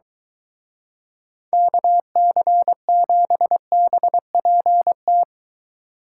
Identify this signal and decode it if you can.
Morse code, heard as a keyed tone, about 23 wpm, 700 Hz. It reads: KC7BPT